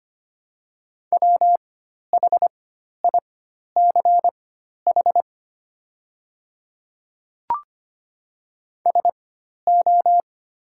Morse code, heard as a keyed tone, about 25 wpm, spaced as for 12 wpm, 700 Hz. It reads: WHICH SO
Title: 4x Character Spacing - Top 100 Words - 25wpm